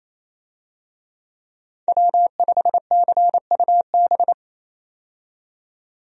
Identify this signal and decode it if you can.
Morse code, heard as a keyed tone, about 28 wpm, 700 Hz. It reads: W5CUB
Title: USA Call Signs - 5 Characters - 28wpm